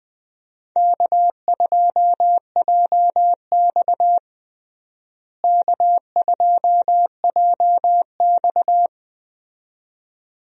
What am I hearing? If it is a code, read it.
Morse code, heard as a keyed tone, about 20 wpm, 700 Hz. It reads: K2JX K2JX
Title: USA Call Signs - 4 Characters - Repeated 2x - 20wpm